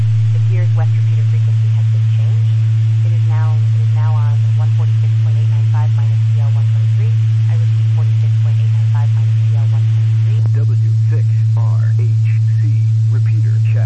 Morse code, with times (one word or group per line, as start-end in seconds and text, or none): none